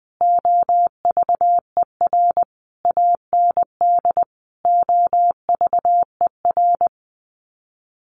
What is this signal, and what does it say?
Morse code, heard as a keyed tone, about 20 wpm, 700 Hz. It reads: OVER AND OVER
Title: Sets of 5 Words Encore 20wpm